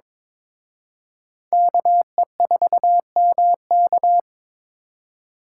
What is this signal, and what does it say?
Morse code, heard as a keyed tone, about 26 wpm, 700 Hz. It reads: KE4MK
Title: Speed Racing - USA 2x2 Call Signs 22wpm